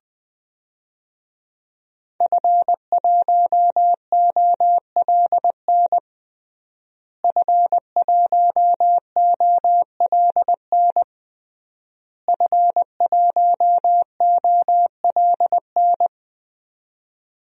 Morse code, heard as a keyed tone, about 20 wpm, 700 Hz. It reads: F1OLN F1OLN F1OLN
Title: DXCC-100 Call Signs - 5 Characters - Repeated 3x - 20wpm